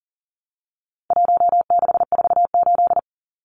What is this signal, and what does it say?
Morse code, heard as a keyed tone, about 40 wpm, 700 Hz. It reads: 1648
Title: Numbers - 4-Digit 40wpm